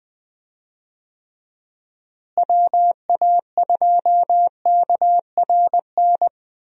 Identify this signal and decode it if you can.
Morse code, heard as a keyed tone, about 21 wpm, 700 Hz. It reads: WA2KRN
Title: USA Call Signs - 6 Characters - Repeated 2x - 20wpm